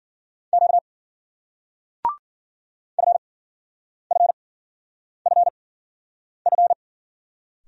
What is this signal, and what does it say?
Morse code, heard as a keyed tone, about 56 wpm, spaced as for 9 wpm, 700 Hz. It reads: Q FFFF